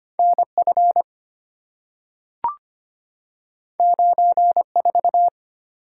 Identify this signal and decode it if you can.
Morse code, heard as a keyed tone, about 25 wpm, 700 Hz. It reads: NF 94
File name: Two Letter-Number Combinations - 25wpm